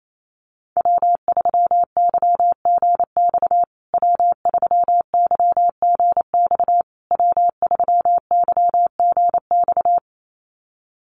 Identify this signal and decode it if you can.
Morse code, heard as a keyed tone, about 28 wpm, 700 Hz. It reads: W3YGX W3YGX W3YGX
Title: USA Call Signs Repeated 3x 28wpm